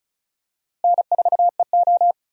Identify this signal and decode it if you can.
Morse code, heard as a keyed tone, about 36 wpm, 700 Hz. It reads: N4EO
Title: USA Call Signs - 4 Characters - Repeated 2x - 35wpm